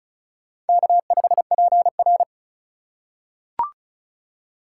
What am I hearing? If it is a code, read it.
Morse code, heard as a keyed tone, about 36 wpm, 700 Hz. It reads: K5PR
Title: USA Call Signs - 4 Characters - 3x Character Spacing - 35wpm